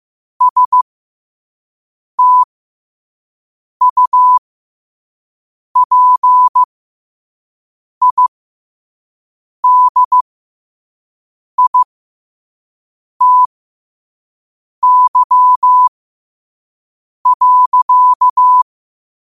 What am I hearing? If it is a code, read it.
Morse code, heard as a keyed tone, about 15 wpm, spaced as for 5 wpm, 1000 Hz. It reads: STUPIDITY.
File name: 5wpm